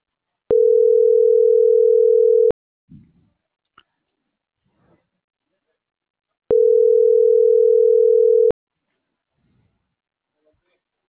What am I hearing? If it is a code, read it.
Ringback tone (North America), 440 + 480 Hz.